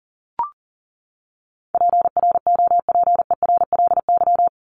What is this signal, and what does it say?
Morse code, heard as a keyed tone, about 40 wpm, 700 Hz. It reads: PROPERLY
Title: Top 1001-1250 Words 40wpm